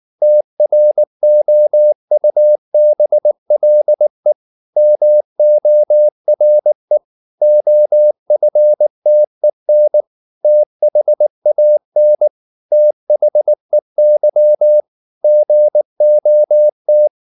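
Morse code, heard as a keyed tone, about 19 wpm, 600 Hz. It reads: TROUBLE MORE OFTEN THAN THEY GOT
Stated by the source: Morse code at 19 wpm from CW med Gnister, reading Ken Follett's novel Never in English.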